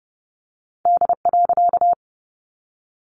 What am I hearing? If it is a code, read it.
Morse code, heard as a keyed tone, about 30 wpm, 700 Hz. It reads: D.
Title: A Sound Of Thunder 30wpm